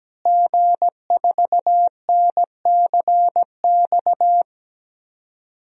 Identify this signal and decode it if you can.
Morse code, heard as a keyed tone, about 17 wpm, 700 Hz. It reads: G4NCX
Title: DXCC-100 Call Signs - 5 Characters - Repeated 2x - 17wpm